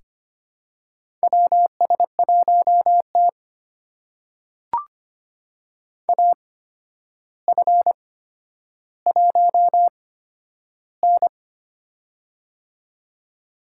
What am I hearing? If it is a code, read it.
Morse code, heard as a keyed tone, about 25 wpm, spaced as for 11 wpm, 700 Hz. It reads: WS1T AF1N
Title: USA Call Signs - 4 Characters - 8x Character Spacing - 25wpm